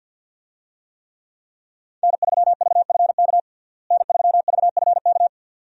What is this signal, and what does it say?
Morse code, heard as a keyed tone, about 52 wpm, 700 Hz. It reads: N3VFK N3VFK
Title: USA Call Signs - 5 Characters - Repeated 2x - 50wpm